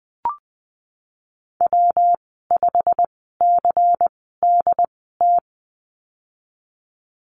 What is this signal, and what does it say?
Morse code, heard as a keyed tone, about 20 wpm, spaced as for 15 wpm, 700 Hz. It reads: W5CDT
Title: 147 Lesson 24g - Letter y - Call Signs 20wpm